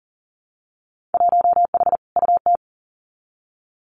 Morse code, heard as a keyed tone, about 38 wpm, 700 Hz. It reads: 1H UT
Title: Winter Field Day Exchanges 40wpm